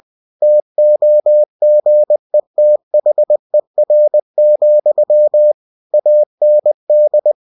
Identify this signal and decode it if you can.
Morse code, heard as a keyed tone, about 20 wpm, 600 Hz. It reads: TOGETHER, AND
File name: cw-books / winnie-the-pooh / winnie-20wpm-page-0025